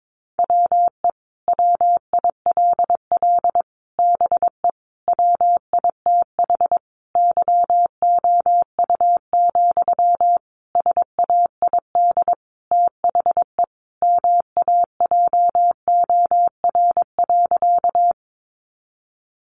Morse code, heard as a keyed tone, about 22 wpm, 700 Hz. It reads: WE WILL BE WITH YOU, SAID THE MAJOR.
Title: Sentences from Top 600 Words 22wpm